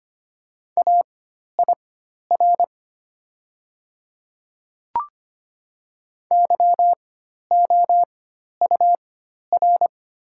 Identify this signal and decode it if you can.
Morse code, heard as a keyed tone, about 25 wpm, spaced as for 12 wpm, 690 Hz. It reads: AIR YOUR
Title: 4x Character Spacing - Top 200 Words - 25wpm